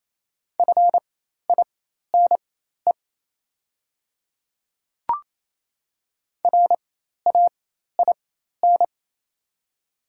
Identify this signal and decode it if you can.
Morse code, heard as a keyed tone, about 27 wpm, spaced as for 13 wpm, 700 Hz. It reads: FINE RAIN
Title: 4x Character Spacing - Top 500 Words - 28wpm